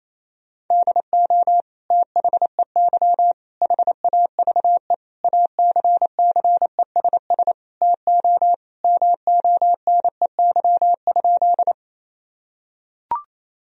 Morse code, heard as a keyed tone, about 28 wpm, 700 Hz. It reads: DO THEY HAVE ACCESS TO MONEY?
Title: Sentences from Top 1000 Words - Review Part 4 - 28wpm